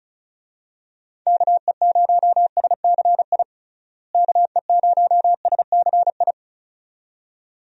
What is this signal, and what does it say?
Morse code, heard as a keyed tone, about 34 wpm, 700 Hz. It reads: KE0SCI KE0SCI
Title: USA Call Signs - 6 Characters - Repeated 2x - 35wpm